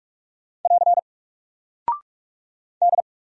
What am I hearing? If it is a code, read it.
Morse code, heard as a keyed tone, about 48 wpm, 700 Hz. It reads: <AR> D
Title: Single Character (Letter:Number:Punctuation:Prosign) Alt 45wpm